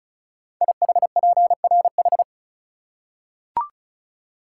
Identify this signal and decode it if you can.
Morse code, heard as a keyed tone, about 36 wpm, 700 Hz. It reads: IHPRH
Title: Code Groups - 5 Letters - 35wpm